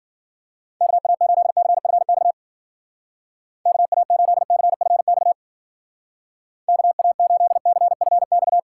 Morse code, heard as a keyed tone, about 50 wpm, 700 Hz. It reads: KA8CFX KA8CFX KA8CFX
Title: Speed Racing - USA Call Signs 40wpm